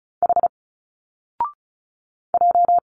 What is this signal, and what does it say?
Morse code, heard as a keyed tone, about 34 wpm, 700 Hz. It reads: H J
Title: Single Letters 35wpm